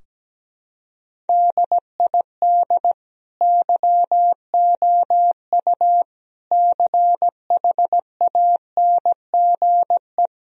Morse code, heard as a keyed tone, about 17 wpm, 700 Hz. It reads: DID YOU CHANGE
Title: Sentences from Top 1000 Words - Review Part 2 - 17wpm